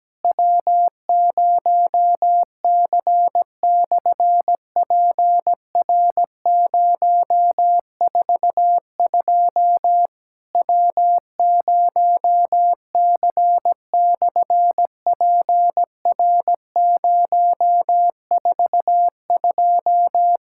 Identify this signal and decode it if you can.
Morse code, heard as a keyed tone, about 17 wpm, 700 Hz. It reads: W0C/PR042 W0C/PR042
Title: SOTA QSO Practice 17wpm